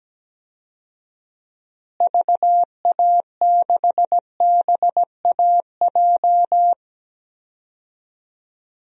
Morse code, heard as a keyed tone, about 17 wpm, 700 Hz. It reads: VA6BAJ
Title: DXCC-100 Call Signs - 6 Characters - Repeated 3x - 17wpm